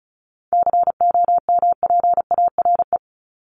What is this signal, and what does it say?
Morse code, heard as a keyed tone, about 34 wpm, 700 Hz. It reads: COMPARE
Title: Top 1251-1500 Words 35wpm